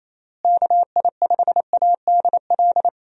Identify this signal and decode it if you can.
Morse code, heard as a keyed tone, about 28 wpm, 700 Hz. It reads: KI5ADL
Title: USA Call Signs - 6 Characters - Repeated 3x - 28wpm